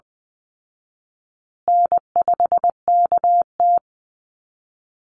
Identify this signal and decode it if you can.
Morse code, heard as a keyed tone, about 20 wpm, 700 Hz. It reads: N5KT
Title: Easy Call Signs Repeated 3x - 20wpm